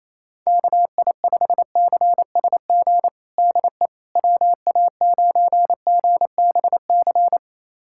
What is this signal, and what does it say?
Morse code, heard as a keyed tone, about 27 wpm, 700 Hz. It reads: KI5CSG DE WA9GBC
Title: USA Callsign DE Callsign - 6 Characters - 28wpm